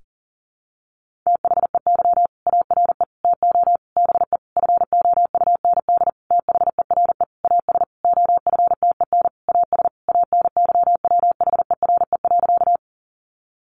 Morse code, heard as a keyed tone, about 40 wpm, 700 Hz. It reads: THEY ARE TO BE FOUND THERE AS OFTEN AS ANYWHERE.
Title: Sentences from Top 1400 Words 40wpm